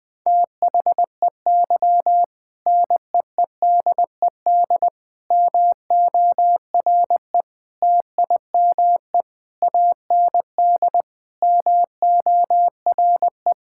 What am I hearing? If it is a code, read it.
Morse code, heard as a keyed tone, about 20 wpm, 700 Hz. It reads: THEY NEEDED MORE TIME AND MORE
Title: Sentences with 2-Word Phrases - 20wpm